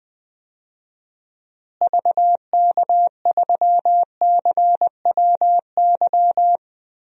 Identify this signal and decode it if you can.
Morse code, heard as a keyed tone, about 20 wpm, 700 Hz. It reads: VK3CWY
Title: DXCC-100 Call Signs - 6 Characters - Repeated 2x - 20wpm